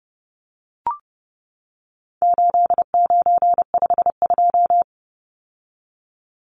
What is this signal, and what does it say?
Morse code, heard as a keyed tone, about 30 wpm, 700 Hz. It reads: 8952
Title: Numbers - 4-Digit 30wpm